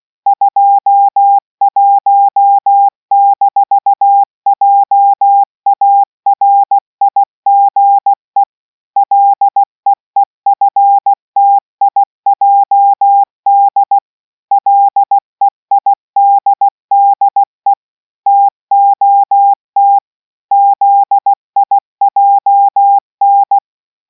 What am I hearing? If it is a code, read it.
Morse code, heard as a keyed tone, about 16 wpm, 800 Hz. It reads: 21-JARIGE LEEFTIJD LEIDDE TOT ZIJN